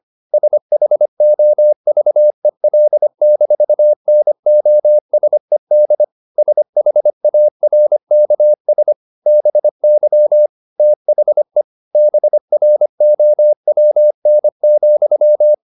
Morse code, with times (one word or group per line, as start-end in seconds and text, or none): 0.3-6.0 SHOVEL-NOSED
6.4-8.9 SHARKS
9.3-10.5 BY
10.8-11.6 THE
11.9-15.5 BROWN,